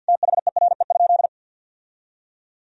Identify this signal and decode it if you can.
Morse code, heard as a keyed tone, about 50 wpm, 700 Hz. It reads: THERE?
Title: Sentences from Top 1000 Words - Review Part 3 - 50wpm